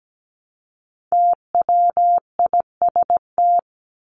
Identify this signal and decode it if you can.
Morse code, heard as a keyed tone, about 17 wpm, 700 Hz. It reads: TWIST
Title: Top 1751-2000 Words 17wpm